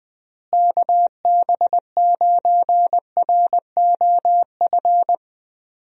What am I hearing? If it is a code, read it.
Morse code, heard as a keyed tone, about 20 wpm, 700 Hz. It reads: KB9ROF
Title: USA Call Signs - 6 Characters - Repeated 3x - 20wpm